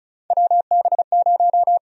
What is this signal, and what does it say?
Morse code, heard as a keyed tone, about 36 wpm, 700 Hz. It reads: WB0
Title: USA Call Sign Prefixes - 35wpm